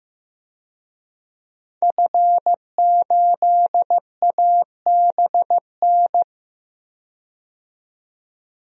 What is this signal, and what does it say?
Morse code, heard as a keyed tone, about 15 wpm, 700 Hz. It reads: F8ABN
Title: DXCC-100 Call Signs - 5 Characters - 15wpm